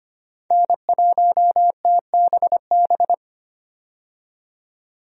Morse code, heard as a keyed tone, about 25 wpm, 690 Hz. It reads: N1TBB
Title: USA Call Signs - 5 Characters - 25wpm